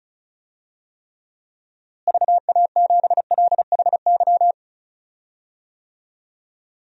Morse code, heard as a keyed tone, about 36 wpm, 700 Hz. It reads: VA7LHY
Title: DXCC-100 Call Signs - 6 Characters - Repeated 3x - 35wpm